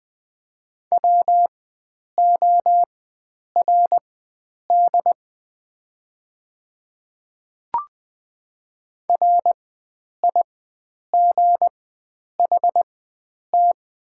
Morse code, heard as a keyed tone, about 20 wpm, spaced as for 9 wpm, 690 Hz. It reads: WORD RIGHT
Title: 4x Character Spacing - Top 500 Words - 20wpm